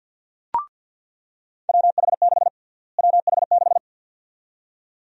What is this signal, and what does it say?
Morse code, heard as a keyed tone, about 46 wpm, 700 Hz. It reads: WH6 WH6
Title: USA Call Sign Prefixes - Repeated 2x - 50wpm